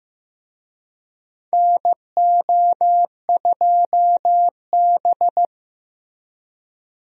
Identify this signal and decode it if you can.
Morse code, heard as a keyed tone, about 15 wpm, 700 Hz. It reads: NO2B
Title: USA Call Signs - 4 Characters - Repeated 3x - 15wpm